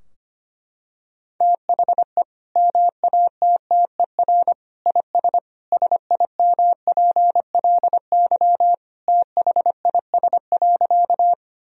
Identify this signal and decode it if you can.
Morse code, heard as a keyed tone, about 25 wpm, 700 Hz. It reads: THE MATTER IS SIMPLY THIS.
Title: Sentences from Top 1000 Words - Review Part 4 - 25wpm